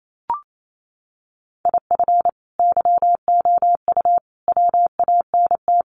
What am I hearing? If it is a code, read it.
Morse code, heard as a keyed tone, about 28 wpm, 700 Hz. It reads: IF YOU WANT
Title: Sets of 3 Words 28wpm